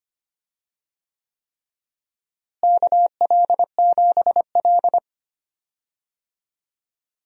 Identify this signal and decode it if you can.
Morse code, heard as a keyed tone, about 25 wpm, 700 Hz. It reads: KL7L
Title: DXCC-100 Call Signs - 4 Characters - Repeated 3x - 25wpm